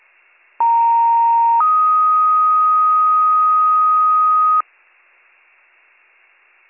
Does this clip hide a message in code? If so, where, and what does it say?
Yes: two-tone page from 0.6 to 4.6 s, heard as a one-second tone, then a three-second tone, about 930 then 1250 Hz.